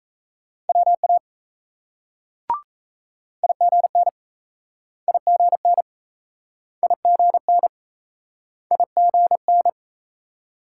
Morse code, heard as a keyed tone, about 39 wpm, 700 Hz. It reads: WA IGN IGN IGN IGN